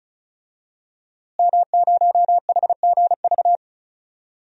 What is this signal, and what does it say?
Morse code, heard as a keyed tone, about 34 wpm, 700 Hz. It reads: M0HGV
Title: DXCC-100 Call Signs - 5 Characters - Repeated 3x - 35wpm